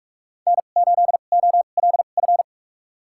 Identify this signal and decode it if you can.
Morse code, heard as a keyed tone, about 46 wpm, spaced as for 32 wpm, 700 Hz. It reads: N8OLF
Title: USA Call Signs - 5 Characters - 2x Character Spacing - 45wpm